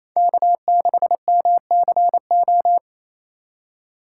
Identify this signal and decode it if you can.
Morse code, heard as a keyed tone, about 28 wpm, 700 Hz. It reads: K6MCO
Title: USA Call Signs - 5 Characters - 8x Character Spacing - 28wpm